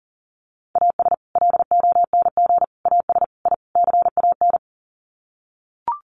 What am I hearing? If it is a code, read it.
Morse code, heard as a keyed tone, about 40 wpm, 700 Hz. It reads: AS LONG AS I CAN
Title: Sets of 5 Words Encore2 40wpm